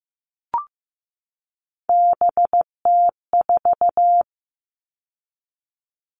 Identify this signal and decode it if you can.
Morse code, heard as a keyed tone, about 15 wpm, 700 Hz. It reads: BT4
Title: All Cut Numbers - 3-Digit 15wpm